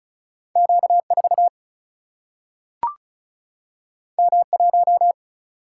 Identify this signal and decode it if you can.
Morse code, heard as a keyed tone, about 36 wpm, 700 Hz. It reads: Q4 M1
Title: Single-Letter Single-Number 35wpm